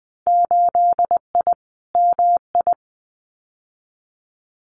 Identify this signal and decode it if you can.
Morse code, heard as a keyed tone, about 20 wpm, 700 Hz. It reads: 8I MI
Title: Winter Field Day Exchanges Encore 20wpm